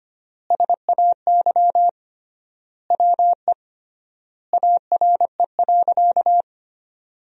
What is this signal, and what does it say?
Morse code, heard as a keyed tone, about 25 wpm, 700 Hz. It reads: SAY WE ARE.
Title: 3x Word-Spacing - Sentences from Top 200 Words - 25wpm